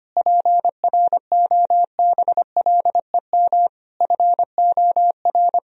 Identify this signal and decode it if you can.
Morse code, heard as a keyed tone, about 25 wpm, 700 Hz. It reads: PROBLEM FOR
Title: Sentences from Top 1000 Words - Review Part 2 - 25wpm